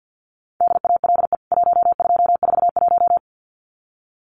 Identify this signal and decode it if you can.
Morse code, heard as a keyed tone, about 50 wpm, 700 Hz. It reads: DALE 1241
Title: CWT Exchange 50wpm